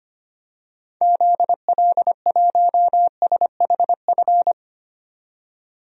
Morse code, heard as a keyed tone, about 25 wpm, 700 Hz. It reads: ZL1SHF
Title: DXCC-100 Call Signs - 6 Characters - 25wpm